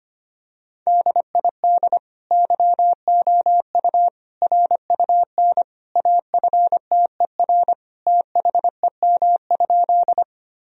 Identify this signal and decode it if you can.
Morse code, heard as a keyed tone, about 25 wpm, 700 Hz. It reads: DID YOU RUN AFTER THEM?
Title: Sentences from Top 1000 Words - Review Part 1 - 25wpm